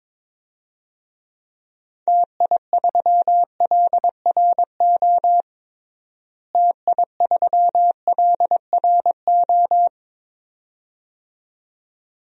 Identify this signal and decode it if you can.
Morse code, heard as a keyed tone, about 22 wpm, 700 Hz. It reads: TI3LRO TI3LRO
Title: DXCC-100 Call Signs - 6 Characters - Repeated 2x - 22wpm